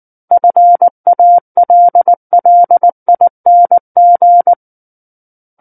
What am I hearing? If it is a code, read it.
Morse code, heard as a keyed tone, about 19 wpm, 700 Hz. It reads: FALLING
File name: Book One / 10-Chapter 10 - 19 WPM